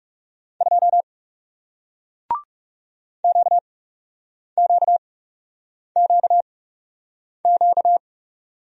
Morse code, heard as a keyed tone, about 41 wpm, spaced as for 8 wpm, 700 Hz. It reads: 2 QQQQ